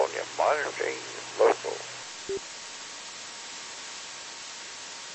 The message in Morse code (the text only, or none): none